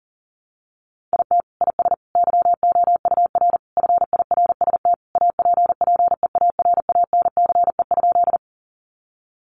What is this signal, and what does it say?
Morse code, heard as a keyed tone, about 40 wpm, 700 Hz. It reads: IT IS YOUR FIRST APPEARANCE?
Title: Sentences from Top 1100 Words 40wpm